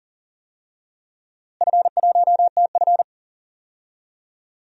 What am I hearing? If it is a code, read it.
Morse code, heard as a keyed tone, about 39 wpm, 700 Hz. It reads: F1TF
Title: DXCC-100 Call Signs - 4 Characters - Repeated 2x - 40wpm